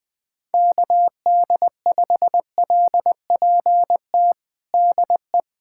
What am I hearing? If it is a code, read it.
Morse code, heard as a keyed tone, about 20 wpm, 700 Hz. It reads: KD5LPT DE
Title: USA Callsign DE Callsign 20wpm